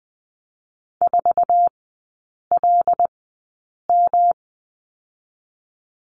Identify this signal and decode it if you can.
Morse code, heard as a keyed tone, about 19 wpm, 700 Hz. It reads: 4 L M
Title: 114 Lesson 19e - Letter w - Three Characters 20wpm